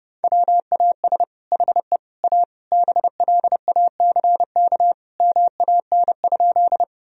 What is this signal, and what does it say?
Morse code, heard as a keyed tone, about 30 wpm, 700 Hz. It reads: WAS HE A BLACK MAN?
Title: Sentences from Top 1000 Words - Review Part 2 - 30wpm